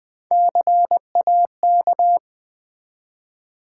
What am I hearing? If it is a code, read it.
Morse code, heard as a keyed tone, about 21 wpm, 700 Hz. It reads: CAK
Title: Build A Word - 1 Letter At A Time - 4-Letter Words - 20wpm